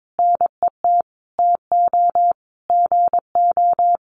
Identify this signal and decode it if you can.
Morse code, heard as a keyed tone, about 22 wpm, 700 Hz. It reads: NET TO GO
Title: Sentences from Top 1400 Words 22wpm